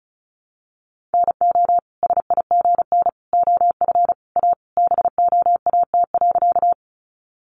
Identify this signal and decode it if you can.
Morse code, heard as a keyed tone, about 34 wpm, 700 Hz. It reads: NO SIGN OF A BOAT.
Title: Sentences from Top 1000 Words 35wpm